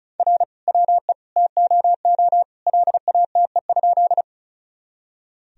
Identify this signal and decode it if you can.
Morse code, heard as a keyed tone, about 34 wpm, 700 Hz. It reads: R WE TOO LATE?
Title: Sentences from Top 300 Words and Abbreviations 35wpm